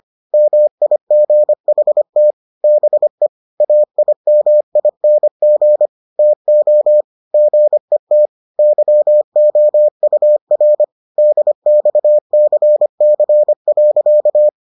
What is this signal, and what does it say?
Morse code, heard as a keyed tone, about 25 wpm, 600 Hz. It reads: MIGHT BE AIMING TO GET YOUR DXCC.